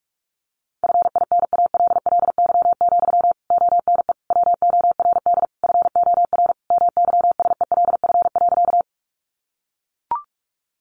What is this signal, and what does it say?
Morse code, heard as a keyed tone, about 44 wpm, 700 Hz. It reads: FINALLY, ONE WORD FOR MYSELF.
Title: Sentences from Top 1300 Words 45wpm